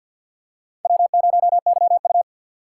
Morse code, heard as a keyed tone, about 49 wpm, 700 Hz. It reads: W0YU
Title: USA Call Signs - 3x Character Spacing - 50wpm